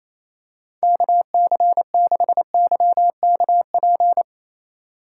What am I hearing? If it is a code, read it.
Morse code, heard as a keyed tone, about 28 wpm, 700 Hz. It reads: KC6YKP
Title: USA Call Signs - 6 Characters - 3x Character Spacing - 28wpm